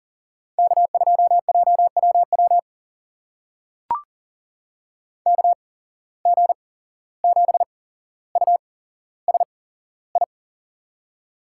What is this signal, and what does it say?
Morse code, heard as a keyed tone, about 40 wpm, spaced as for 16 wpm, 700 Hz. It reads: K2JWW KG7USI
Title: USA Call Signs - 8x Character Spacing - 40wpm